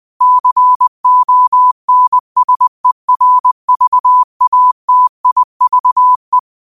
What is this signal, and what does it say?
Morse code, heard as a keyed tone, about 20 wpm, 1000 Hz. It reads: CONSERVATIVE
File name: qotdfiles / 20wpm